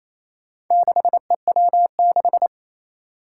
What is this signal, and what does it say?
Morse code, heard as a keyed tone, about 28 wpm, 700 Hz. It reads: 6EW6
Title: Four Challenging Letter-Number Combinations 28wpm